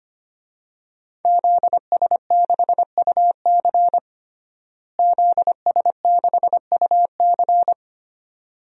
Morse code, heard as a keyed tone, about 25 wpm, 700 Hz. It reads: ZS6UC ZS6UC
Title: DXCC-100 Call Signs - 5 Characters - Repeated 2x - 25wpm